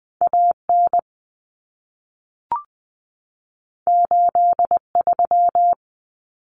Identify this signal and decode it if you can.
Morse code, heard as a keyed tone, about 20 wpm, 700 Hz. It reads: AN 83
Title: Common Cut Numbers - 2-Digit 20wpm